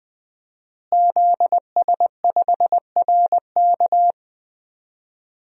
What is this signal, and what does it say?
Morse code, heard as a keyed tone, about 20 wpm, 700 Hz. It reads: ZS5RK
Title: DXCC-100 Call Signs - 5 Characters - Repeated 2x - 20wpm